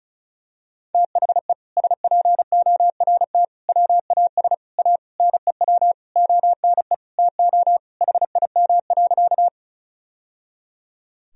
Morse code, heard as a keyed tone, about 36 wpm, 700 Hz. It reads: THE SPORT WAS A NEW ONE TO HIM.
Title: Sentences from Top 1000 Words - Review Part 4 - 35wpm